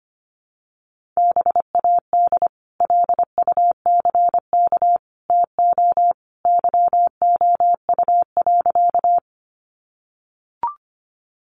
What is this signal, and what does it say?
Morse code, heard as a keyed tone, about 25 wpm, 700 Hz. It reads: BAD LUCK TO YOU.
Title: Sentences from Top 1500 Words 25wpm